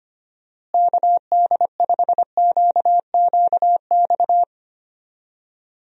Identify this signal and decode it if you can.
Morse code, heard as a keyed tone, about 25 wpm, 700 Hz. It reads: KD5QQX
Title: USA Call Signs - 6 Characters - Repeated 2x - 25wpm